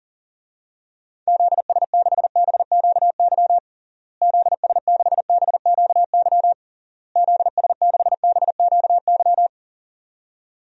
Morse code, heard as a keyed tone, about 37 wpm, 700 Hz. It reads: ZS6BQY ZS6BQY ZS6BQY
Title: DXCC-100 Call Signs - 6 Characters - Repeated 3x - 40wpm